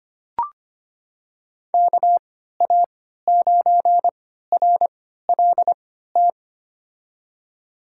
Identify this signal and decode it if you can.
Morse code, heard as a keyed tone, about 25 wpm, spaced as for 14 wpm, 700 Hz. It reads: KA9RLT
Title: USA Call Signs - 6 Characters - 3x Character Spacing - 25wpm